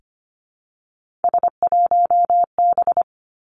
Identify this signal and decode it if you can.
Morse code, heard as a keyed tone, about 25 wpm, 700 Hz. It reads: S1B
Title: Three Challenging Letter-Number Combinations 25wpm